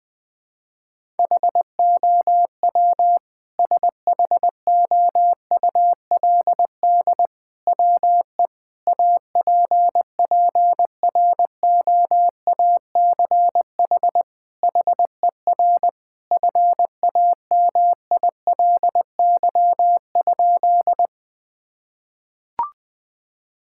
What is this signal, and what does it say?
Morse code, heard as a keyed tone, about 20 wpm, 700 Hz. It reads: HOW SHOULD WE APPROACH HER FAMILY?
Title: Sentences from Top 1000 Words - Review Part 6 - 20wpm